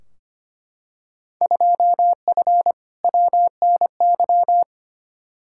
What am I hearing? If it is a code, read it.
Morse code, heard as a keyed tone, about 25 wpm, 700 Hz. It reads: 2F WNY
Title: Field Day Exchanges Encore - 25wpm